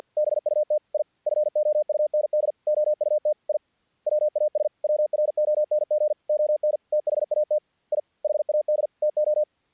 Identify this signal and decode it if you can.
Morse code, heard as a keyed tone, about 48 wpm, 600 Hz. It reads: BUT I FOUND OUT I WAS WRONG ON THAT I HAD TO
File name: archived / CRSnet